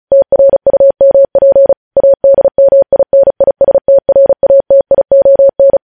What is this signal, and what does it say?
Morse code, heard as a keyed tone, about 36 wpm, 550 Hz. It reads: TRUMP ADMINISTRATION
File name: podcast / News headlines-1x-35x35